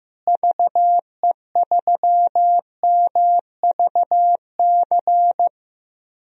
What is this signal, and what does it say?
Morse code, heard as a keyed tone, about 15 wpm, 700 Hz. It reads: VE3MVC
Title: DXCC-100 Call Signs - 6 Characters - Repeated 3x - 15wpm